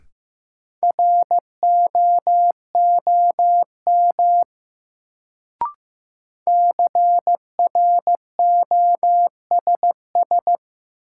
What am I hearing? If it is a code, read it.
Morse code, heard as a keyed tone, about 15 wpm, 700 Hz. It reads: ROOM CROSS
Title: Sentences from Top 300 Words - New Word Drill - 15wpm